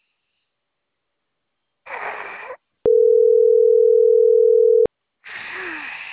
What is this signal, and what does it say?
Ringback tone (North America), 440 + 480 Hz.